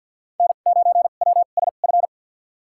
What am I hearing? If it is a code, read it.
Morse code, heard as a keyed tone, about 53 wpm, spaced as for 37 wpm, 700 Hz. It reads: N9WSF
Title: USA Call Signs - 5 Characters - 2x Character Spacing - 50wpm